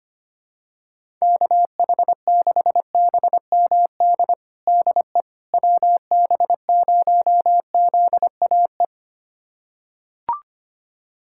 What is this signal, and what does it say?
Morse code, heard as a keyed tone, about 25 wpm, 700 Hz. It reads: KH6BMD DE WB0ZAE
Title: USA Callsign DE Callsign - 6 Characters - 25wpm